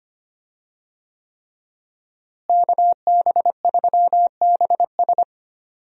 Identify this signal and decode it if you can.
Morse code, heard as a keyed tone, about 25 wpm, 700 Hz. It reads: KB3BS
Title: USA Call Signs - 5 Characters - Repeated 3x - 25wpm